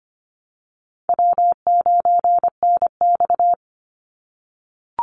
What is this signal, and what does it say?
Morse code, heard as a keyed tone, about 25 wpm, 700 Hz. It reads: W9NX
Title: Easy Call Signs - 25wpm